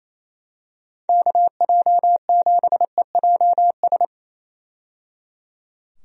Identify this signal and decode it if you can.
Morse code, heard as a keyed tone, about 28 wpm, 700 Hz. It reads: KJ7EJS
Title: USA Call Signs - 6 Characters - 28wpm